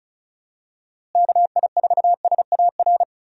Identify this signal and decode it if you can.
Morse code, heard as a keyed tone, about 36 wpm, 700 Hz. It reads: KI4SAR
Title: USA Call Signs - 6 Characters - Repeated 3x - 35wpm